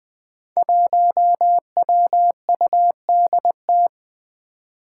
Morse code, heard as a keyed tone, about 21 wpm, 700 Hz. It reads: 1WUDT
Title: Five Challenging Letter-Number Combinations 20wpm